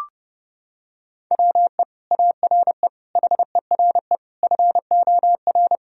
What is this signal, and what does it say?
Morse code, heard as a keyed tone, about 30 wpm, 700 Hz. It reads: WE ARE HERE FOR
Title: Sentences from Top 1000 Words - Review Part 4 - 30wpm